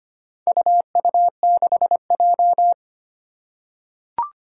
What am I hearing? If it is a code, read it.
Morse code, heard as a keyed tone, about 25 wpm, 700 Hz. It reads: UU6J
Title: Four Challenging Letter-Number Combinations 25wpm